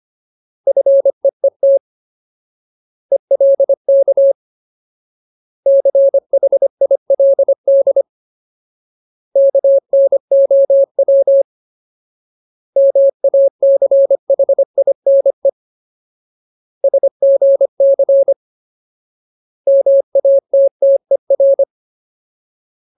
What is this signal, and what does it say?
Morse code, heard as a keyed tone, about 25 wpm, 550 Hz. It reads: FEET ELK CHILD KNOW MACHINE SGC MATTER
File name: Ultimate Word List / 25WPM Ultimate